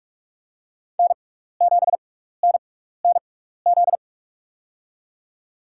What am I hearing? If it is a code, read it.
Morse code, heard as a keyed tone, about 42 wpm, spaced as for 15 wpm, 700 Hz. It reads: N7NNZ
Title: USA Call Signs - 6x Character Spacing - 45wpm